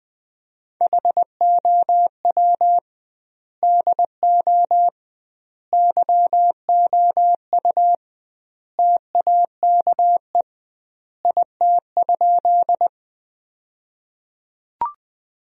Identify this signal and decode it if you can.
Morse code, heard as a keyed tone, about 20 wpm, 700 Hz. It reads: HOW DO YOU TAKE IT?
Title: 2x Word-Spacing - Sentences from Top 200 Words - 20wpm